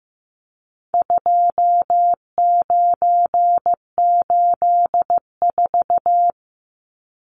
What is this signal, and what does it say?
Morse code, heard as a keyed tone, about 15 wpm, 700 Hz. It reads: 2984
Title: Numbers - 4-Digit 15wpm